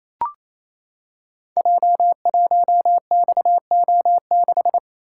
Morse code, heard as a keyed tone, about 28 wpm, 700 Hz. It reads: J1XO6
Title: Code Groups - 5 Letter:Numbers - 28wpm